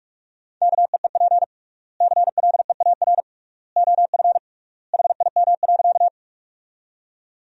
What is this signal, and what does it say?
Morse code, heard as a keyed tone, about 45 wpm, 700 Hz. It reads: KEEP CLEAR OF HIM.
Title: 3x Word-Spacing - Sentences from Top 500 Words - 45wpm